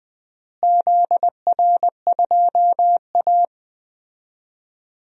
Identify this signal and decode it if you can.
Morse code, heard as a keyed tone, about 20 wpm, 690 Hz. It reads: ZR2A
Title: DXCC-100 Call Signs - 4 Characters - Repeated 3x - 20wpm